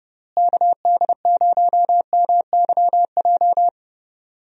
Morse code, heard as a keyed tone, about 30 wpm, 700 Hz. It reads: KD0MYJ
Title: USA Call Signs - 6 Characters - Repeated 2x - 30wpm